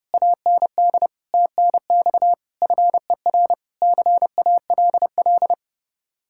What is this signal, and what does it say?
Morse code, heard as a keyed tone, about 31 wpm, 700 Hz. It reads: AND TNX FER CALL